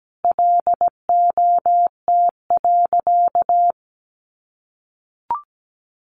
Morse code, heard as a keyed tone, about 17 wpm, 700 Hz. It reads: LOT.
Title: Sentences from Top 1500 Words 17wpm